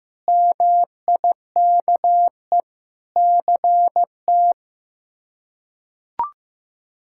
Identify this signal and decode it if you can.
Morse code, heard as a keyed tone, about 15 wpm, 700 Hz. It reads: MIKE CT
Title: K1USN SST QSO Practice - 15wpm